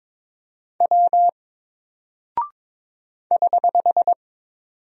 Morse code, heard as a keyed tone, about 22 wpm, 700 Hz. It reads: W <HH>
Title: Single Character (Letter:Number:Punctuation:Prosign) Alt 22wpm